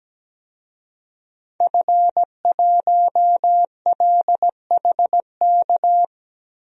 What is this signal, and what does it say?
Morse code, heard as a keyed tone, about 17 wpm, 700 Hz. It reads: F1LHK
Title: DXCC-100 Call Signs - 5 Characters - Repeated 2x - 17wpm